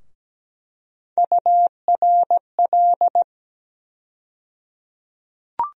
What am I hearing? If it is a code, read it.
Morse code, heard as a keyed tone, about 17 wpm, 700 Hz. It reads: URL
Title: USA Call Sign Postfixes - 3 Letters 17wpm